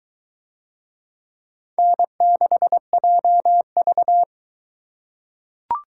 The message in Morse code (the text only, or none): N6JV